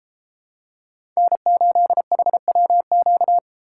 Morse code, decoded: N8HWQ

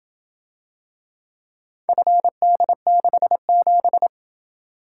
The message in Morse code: FD67

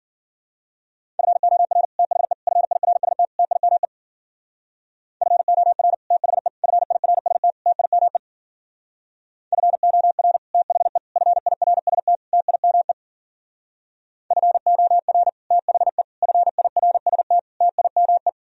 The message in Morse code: FOR THE FIRST TIME FOR THE FIRST TIME FOR THE FIRST TIME FOR THE FIRST TIME

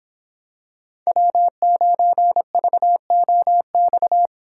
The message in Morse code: W9VOX

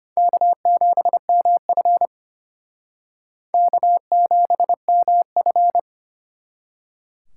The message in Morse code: K7MF K7MF

